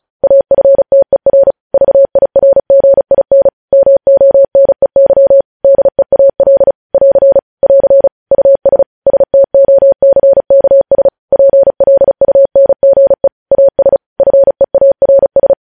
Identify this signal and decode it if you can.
Morse code, heard as a keyed tone, about 36 wpm, 550 Hz. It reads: AFTER VIRGIN MONEY DEAL <AR> <AR> US STOCKS PLUNGE AS FEARS
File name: podcast / News headlines-1x-35x35